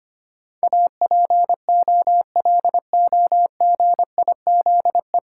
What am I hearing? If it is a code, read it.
Morse code, heard as a keyed tone, about 25 wpm, 690 Hz. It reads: APOLOGIZE
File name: practice / 25wpm / 9 Letter Words - First 3 Letters Given - 25wpm